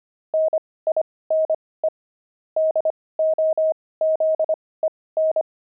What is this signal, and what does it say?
Morse code, heard as a keyed tone, about 25 wpm, spaced as for 18 wpm, 630 Hz. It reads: NINE DOZEN
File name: pangram-fs-18-cs-25-nine